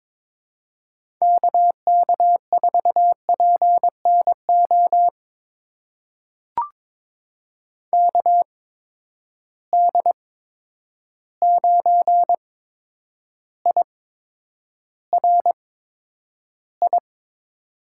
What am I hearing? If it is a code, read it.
Morse code, heard as a keyed tone, about 22 wpm, spaced as for 9 wpm, 700 Hz. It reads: KK4PNO KD9IRI
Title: USA Call Signs - 6 Characters - 8x Character Spacing - 22wpm